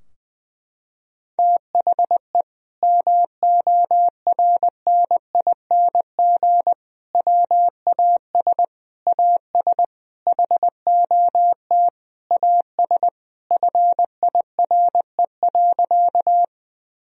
Morse code, heard as a keyed tone, about 20 wpm, 700 Hz. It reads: THE MORNING WAS AS HOT AS FIRE.
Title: Sentences from Top 1000 Words - Review Part 3 - 20wpm